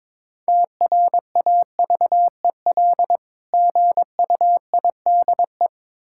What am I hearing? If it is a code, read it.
Morse code, heard as a keyed tone, about 22 wpm, 700 Hz. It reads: TRAVEL GUIDE
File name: Difficult Words with Hints - 22wpm